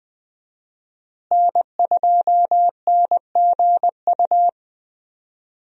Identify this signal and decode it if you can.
Morse code, heard as a keyed tone, about 20 wpm, 700 Hz. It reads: N2NGU
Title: USA Call Signs - 5 Characters - Repeated 3x - 20wpm